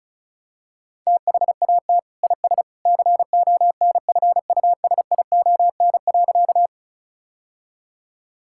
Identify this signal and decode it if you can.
Morse code, heard as a keyed tone, about 34 wpm, 700 Hz. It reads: THAT IS CONFUSION.
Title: Sentences from Top 1600 Words 35wpm